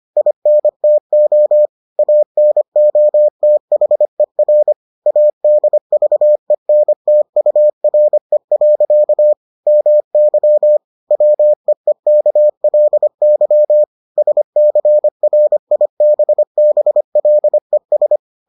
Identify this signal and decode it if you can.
Morse code, heard as a keyed tone, about 25 wpm, 600 Hz. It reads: INTO ANOTHER ADVENTURE. MY WEEKLY SCRIBBLES